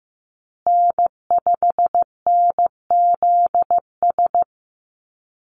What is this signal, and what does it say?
Morse code, heard as a keyed tone, about 15 wpm, 700 Hz. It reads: N5NZS
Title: USA Call Signs Repeated 3x 15wpm